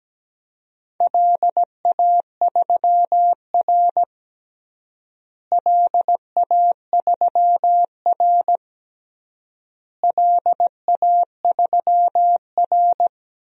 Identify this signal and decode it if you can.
Morse code, heard as a keyed tone, about 17 wpm, 700 Hz. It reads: LA3R LA3R LA3R
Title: DXCC-100 Call Signs - 4 Characters - Repeated 3x - 17wpm